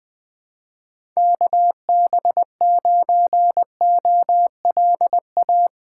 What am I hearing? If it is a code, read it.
Morse code, heard as a keyed tone, about 21 wpm, 700 Hz. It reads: KB9OLA
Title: USA Call Signs - 4x Character Spacing - 20wpm